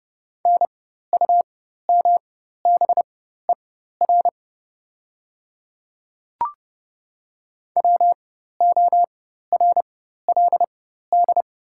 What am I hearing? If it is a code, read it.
Morse code, heard as a keyed tone, about 30 wpm, spaced as for 14 wpm, 700 Hz. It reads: NUMBER WORLD
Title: 4x Character Spacing - Top 200 Words - 30wpm